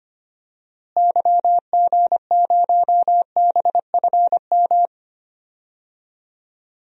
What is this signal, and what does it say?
Morse code, heard as a keyed tone, about 25 wpm, 700 Hz. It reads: YG0BFM
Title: DXCC-100 Call Signs - 6 Characters - Repeated 2x - 25wpm